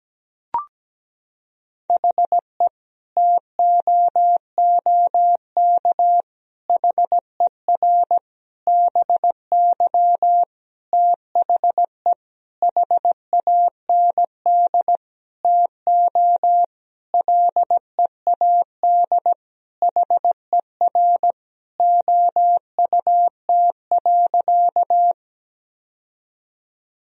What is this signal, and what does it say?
Morse code, heard as a keyed tone, about 17 wpm, 700 Hz. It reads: HE TOOK HER BY THE HAND TO LEAD HER OUT.
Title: Sentences from Top 1000 Words - Review Part 2 - 17wpm